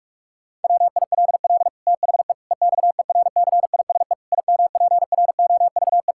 Morse code, heard as a keyed tone, about 46 wpm, 700 Hz. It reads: WILL THE EXERCISE IMPROVE
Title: Sentences from Top 1000 Words - Review Part 6 - 45wpm